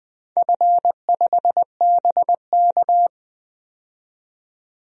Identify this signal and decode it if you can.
Morse code, heard as a keyed tone, about 20 wpm, 700 Hz. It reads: F5BK
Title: DXCC-100 Call Signs - 4 Characters - 20wpm